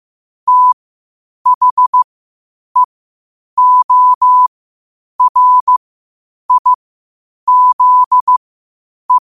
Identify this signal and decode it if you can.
Morse code, heard as a keyed tone, about 15 wpm, spaced as for 8 wpm, 1000 Hz. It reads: THEORIZE
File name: qotdfiles / 8wpm